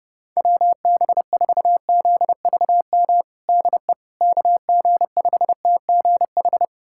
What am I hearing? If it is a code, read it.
Morse code, heard as a keyed tone, about 29 wpm, 700 Hz. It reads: WB4ZVM DE KG5TGH
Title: USA Callsign DE Callsign - 6 Characters - 30wpm